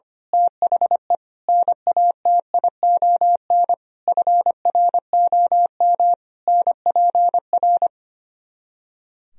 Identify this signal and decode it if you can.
Morse code, heard as a keyed tone, about 24 wpm, 700 Hz. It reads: THE NATION FROM NPR
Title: Speed-Racing - Sets of 5 Words 25wpm